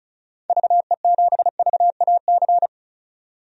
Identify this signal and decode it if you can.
Morse code, heard as a keyed tone, about 36 wpm, 700 Hz. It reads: VE7VAC
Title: DXCC-100 Call Signs - 6 Characters - 35wpm